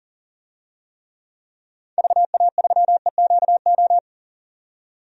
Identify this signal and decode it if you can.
Morse code, heard as a keyed tone, about 40 wpm, 700 Hz. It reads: VA3EQO
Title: DXCC-100 Call Signs - 6 Characters - 40wpm